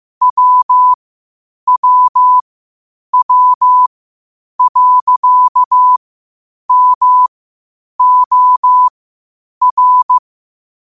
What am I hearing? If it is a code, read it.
Morse code, heard as a keyed tone, about 15 wpm, spaced as for 8 wpm, 1000 Hz. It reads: WWW.MOR
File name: qotdfiles / 8wpm